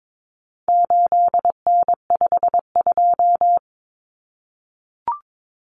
Morse code, heard as a keyed tone, about 22 wpm, 700 Hz. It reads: 8N52
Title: Common Cut Numbers - 4-Digit 22wpm